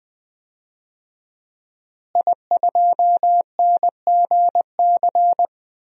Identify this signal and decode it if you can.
Morse code, heard as a keyed tone, about 20 wpm, 700 Hz. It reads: I2NGC
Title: DXCC-100 Call Signs - 5 Characters - Repeated 3x - 20wpm